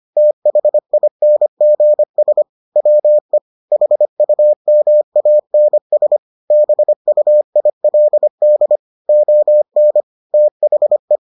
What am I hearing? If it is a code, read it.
Morse code, heard as a keyed tone, about 25 wpm, 600 Hz. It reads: THINGS WE HUMANS BUILD ON THE